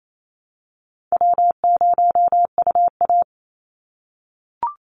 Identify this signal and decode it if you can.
Morse code, heard as a keyed tone, about 28 wpm, 700 Hz. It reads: W0UA
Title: Call Signs - 28wpm